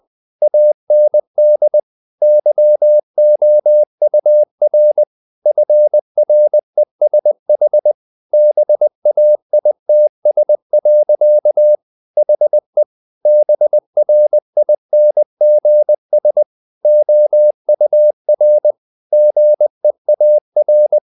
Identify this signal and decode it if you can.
Morse code, heard as a keyed tone, about 20 wpm, 600 Hz. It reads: AND YOUR FRESH BAITS. HE BRINGS OUR GEAR
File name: the old man and the sea / the old man and the sea part 1 20wpm